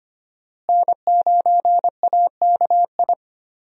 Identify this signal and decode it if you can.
Morse code, heard as a keyed tone, about 25 wpm, 700 Hz. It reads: N9AKI
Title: USA Call Signs - 5 Characters - Repeated 3x - 25wpm